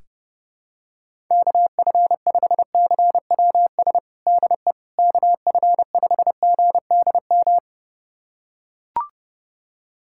Morse code, heard as a keyed tone, about 30 wpm, 700 Hz. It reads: KF5CWS DE KF5GDM
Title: USA Callsign DE Callsign - 6 Characters - 30wpm